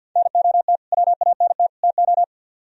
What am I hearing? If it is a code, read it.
Morse code, heard as a keyed tone, about 66 wpm, spaced as for 57 wpm, 700 Hz. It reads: NOT WANT TO
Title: Speed-Racing - Sets of 3 Words 50wpm